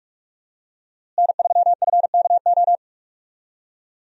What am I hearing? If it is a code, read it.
Morse code, heard as a keyed tone, about 46 wpm, 700 Hz. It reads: N3FKO
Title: USA Call Signs - 5 Characters - Repeated 2x - 45wpm